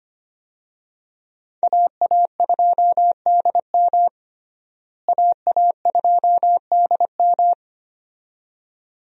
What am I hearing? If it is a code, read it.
Morse code, heard as a keyed tone, about 25 wpm, 700 Hz. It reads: AA2DM AA2DM
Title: USA Call Signs - 5 Characters - Repeated 2x - 25wpm